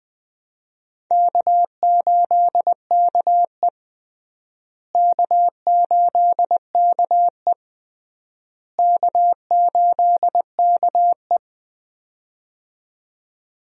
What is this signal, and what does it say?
Morse code, heard as a keyed tone, about 20 wpm, 700 Hz. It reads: K8KE K8KE K8KE
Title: USA Call Signs - 4 Characters - Repeated 3x - 20wpm